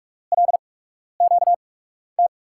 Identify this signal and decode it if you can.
Morse code, heard as a keyed tone, about 44 wpm, spaced as for 12 wpm, 700 Hz. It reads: LQT
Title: USA Call Signs - 8x Character Spacing - 45wpm